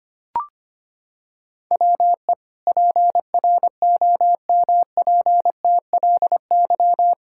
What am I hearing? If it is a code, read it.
Morse code, heard as a keyed tone, about 25 wpm, 700 Hz. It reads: WE PROMPTLY
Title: Pangram Challenge v2 - 25wpm